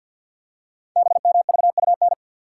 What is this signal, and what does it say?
Morse code, heard as a keyed tone, about 48 wpm, 700 Hz. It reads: BMVUN
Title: Code Groups - 5 Letters - 50wpm